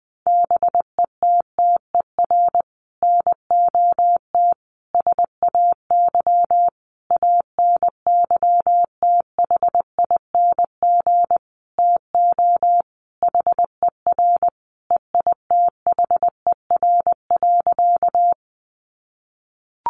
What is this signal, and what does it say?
Morse code, heard as a keyed tone, about 19 wpm, 700 Hz. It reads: BETTER NOT SAY ANYTHING TO HER EITHER.
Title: Sentences from Top 1000 Words 20wpm